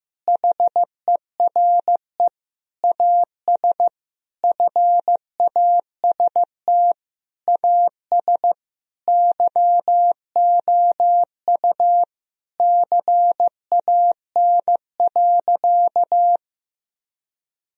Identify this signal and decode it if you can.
Morse code, heard as a keyed tone, about 15 wpm, 700 Hz. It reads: HERE AS FAST AS YOU CAN.
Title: Sentences from Top 1000 Words - Review Part 2 - 15wpm